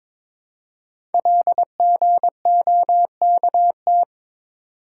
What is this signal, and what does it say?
Morse code, heard as a keyed tone, about 23 wpm, 700 Hz. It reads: LGOKT
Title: Code Groups - 5 Letters - 22wpm